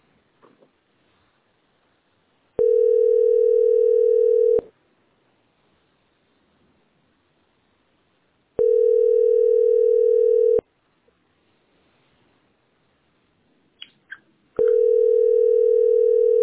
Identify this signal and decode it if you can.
Ringback tone (North America), 440 + 480 Hz.